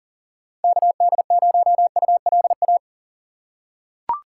Morse code, heard as a keyed tone, about 40 wpm, 700 Hz. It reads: KD0ULA
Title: USA Call Signs - 6 Characters - 3x Character Spacing - 40wpm